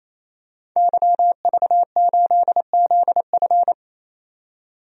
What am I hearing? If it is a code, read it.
Morse code, heard as a keyed tone, about 28 wpm, 700 Hz. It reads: YV8ZF
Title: DXCC-100 Call Signs - 5 Characters - 28wpm